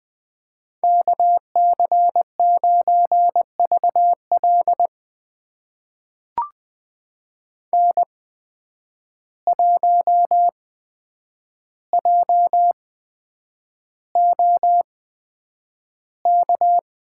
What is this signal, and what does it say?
Morse code, heard as a keyed tone, about 20 wpm, spaced as for 9 wpm, 700 Hz. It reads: KC9VL N1JOK
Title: USA Call Signs - 5 Characters - 8x Character Spacing - 20wpm